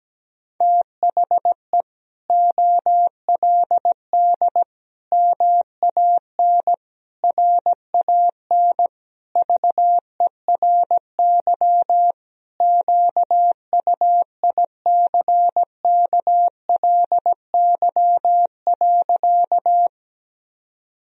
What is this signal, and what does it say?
Morse code, heard as a keyed tone, about 17 wpm, 700 Hz. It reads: THE OLD MAN RAN VERY QUICKLY.
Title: Sentences from Top 1000 Words - Review Part 3 - 17wpm